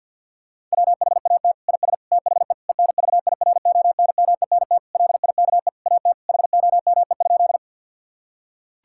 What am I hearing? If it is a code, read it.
Morse code, heard as a keyed tone, about 50 wpm, 700 Hz. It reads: WHAT IS THE ENVIRONMENT LIKE AT HOME?
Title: Sentences from Top 1000 Words - Review Part 5 - 50wpm